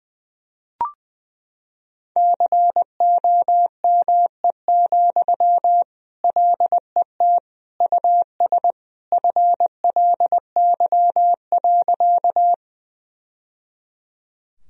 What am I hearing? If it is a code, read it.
Morse code, heard as a keyed tone, about 21 wpm, 700 Hz. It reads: COME, LET US FLY.
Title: Sentences from Top 1000 Words - Review Part 2 - 20wpm